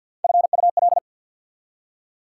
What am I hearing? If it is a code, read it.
Morse code, heard as a keyed tone, about 54 wpm, 690 Hz. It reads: FUL
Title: Prefixes and Suffixes 50wpm